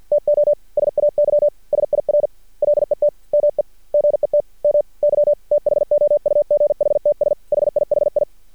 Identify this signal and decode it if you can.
Morse code, heard as a keyed tone, about 51 wpm, 590 Hz. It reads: TO SAY SIR LET ME GET MY THOUGHTS HIHI
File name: archived / CRSnet